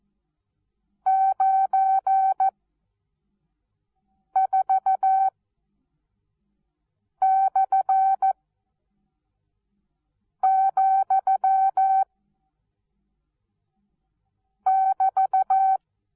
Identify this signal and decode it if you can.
Morse code, heard as a keyed tone, about 14 wpm, spaced as for 4 wpm, 770 Hz. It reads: 94/,<BT>
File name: Onlinelehrgang / morsen / DJ4UF Morsen19